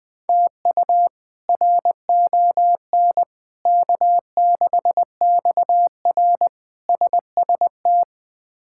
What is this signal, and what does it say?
Morse code, heard as a keyed tone, about 20 wpm, 700 Hz. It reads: TU RON K6XR SST
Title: K1USN SST QSO Practice - 20wpm